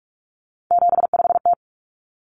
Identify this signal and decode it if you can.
Morse code, heard as a keyed tone, about 44 wpm, 700 Hz. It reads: 75T
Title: Common Cut Numbers - 3-Digit 45wpm